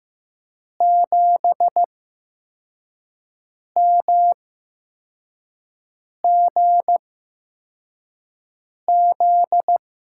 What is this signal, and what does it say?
Morse code, heard as a keyed tone, about 15 wpm, spaced as for 4 wpm, 700 Hz. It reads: 7MGZ
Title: USA Call Signs - 6 Characters - 8x Character Spacing - 15wpm